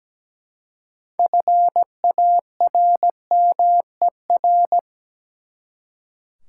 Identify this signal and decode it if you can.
Morse code, heard as a keyed tone, about 17 wpm, 700 Hz. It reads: FARMER
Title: Six Letter Words - 17wpm